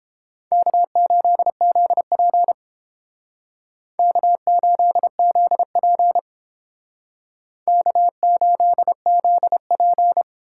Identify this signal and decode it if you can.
Morse code, heard as a keyed tone, about 29 wpm, 700 Hz. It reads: K8ZP K8ZP K8ZP